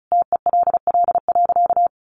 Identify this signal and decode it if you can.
Morse code, heard as a keyed tone, about 34 wpm, 700 Hz. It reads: TELL.
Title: A Sound Of Thunder 35wpm